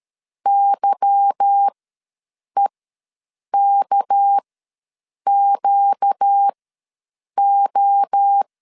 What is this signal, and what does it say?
Morse code, heard as a keyed tone, about 13 wpm, spaced as for 7 wpm, 800 Hz. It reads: YEKQO
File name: LEZIONI / 8 YJ45 / 8 Y ws 12wpm 3409